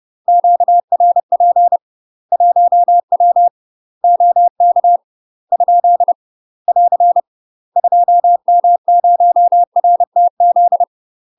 Morse code, heard as a keyed tone, about 30 wpm, 700 Hz. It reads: QRP 1W OK ? <AR> 2M0RTZ
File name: UK MASTER QSOs TRAINING / 30WPM / UK MASTER QSOs 0002